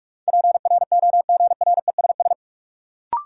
Morse code, heard as a keyed tone, about 45 wpm, 700 Hz. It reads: PROGRESS
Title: Sentences from Top 1400 Words - New Word Drill - 45wpm